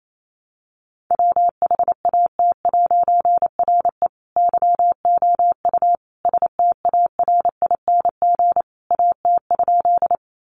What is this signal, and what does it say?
Morse code, heard as a keyed tone, about 28 wpm, 700 Hz. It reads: WHAT'RE YOU STARING AT?
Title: A Sound Of Thunder 28wpm